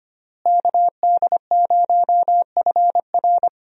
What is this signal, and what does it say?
Morse code, heard as a keyed tone, about 25 wpm, 700 Hz. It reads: KD0FR
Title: USA Call Signs - 5 Characters - Repeated 3x - 25wpm